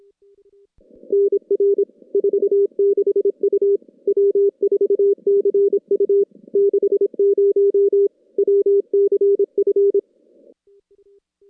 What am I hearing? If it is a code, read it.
Morse code, heard as a keyed tone, about 26 wpm, 400 Hz. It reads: NR 46 U W4CU 60 WCF
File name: contests / CW-SS-k5tr- -2200